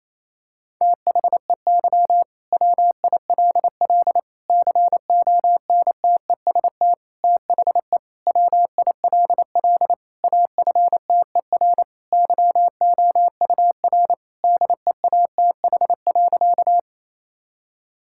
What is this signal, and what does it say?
Morse code, heard as a keyed tone, about 28 wpm, 700 Hz. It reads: THEY WILL CONTEST THE WILL AFTER YOUR DEATH.
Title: Sentences from Top 1600 Words 28wpm